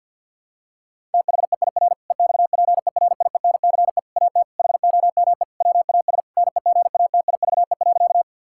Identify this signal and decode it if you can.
Morse code, heard as a keyed tone, about 50 wpm, 700 Hz. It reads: THEIR EXPERIENCE AT HOME WAS NEGATIVE.
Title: Sentences from Top 1000 Words - Review Part 6 - 50wpm